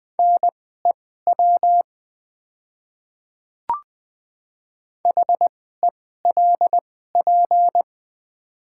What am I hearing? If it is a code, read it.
Morse code, heard as a keyed tone, about 20 wpm, spaced as for 14 wpm, 700 Hz. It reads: NEW HELP
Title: 2x Character Spacing - Top 300 Words - 20wpm